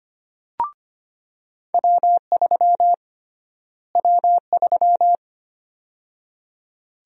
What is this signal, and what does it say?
Morse code, heard as a keyed tone, about 25 wpm, 700 Hz. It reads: W3 W3
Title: USA Call Sign Prefixes - Repeated 2x - 25wpm